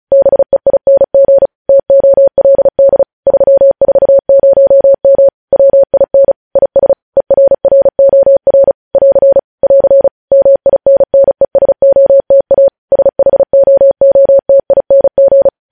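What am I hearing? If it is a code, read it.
Morse code, heard as a keyed tone, about 36 wpm, 550 Hz. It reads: BEING TOLD 340M WIN IS ERROR <AR> <AR> MINNESOTA SHOOTING
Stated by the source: News headlines sent in Morse at 35 WPM.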